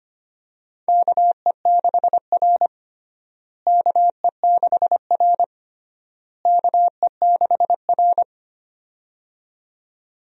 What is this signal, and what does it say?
Morse code, heard as a keyed tone, about 25 wpm, 700 Hz. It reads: KE6R KE6R KE6R
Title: USA Call Signs - 4 Characters - Repeated 3x - 25wpm